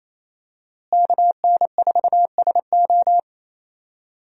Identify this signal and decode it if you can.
Morse code, heard as a keyed tone, about 28 wpm, 700 Hz. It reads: KN4SO
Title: USA Call Signs - 5 Characters - Repeated 3x - 28wpm